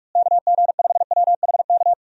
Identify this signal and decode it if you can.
Morse code, heard as a keyed tone, about 43 wpm, 700 Hz. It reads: KG5WHK
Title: USA Call Signs - 6 Characters - Repeated 2x - 45wpm